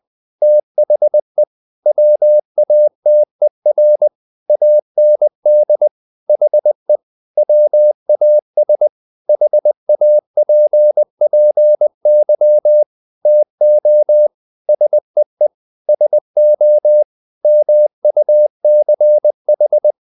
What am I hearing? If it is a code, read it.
Morse code, heard as a keyed tone, about 20 wpm, 600 Hz. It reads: THE WATER AND HE WAS HAPPY TO SEE SO MUCH
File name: the old man and the sea / the old man and the sea part 1 20wpm